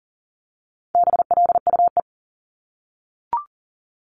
Morse code, heard as a keyed tone, about 38 wpm, 700 Hz. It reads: BLUE
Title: Multi-letter Phonemes 40wpm